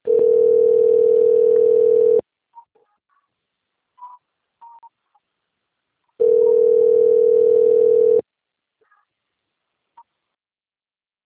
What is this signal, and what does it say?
Ringback tone (North America), 440 + 480 Hz.